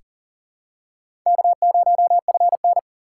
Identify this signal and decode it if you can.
Morse code, heard as a keyed tone, about 40 wpm, 700 Hz. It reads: K0FN
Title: USA Call Signs - 3x Character Spacing - 40wpm